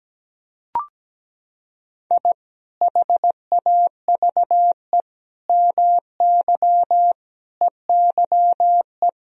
Morse code, heard as a keyed tone, about 17 wpm, 700 Hz. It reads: I HAVE MY EYE
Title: Sentences from Top 1000 Words - Review Part 1 - 17wpm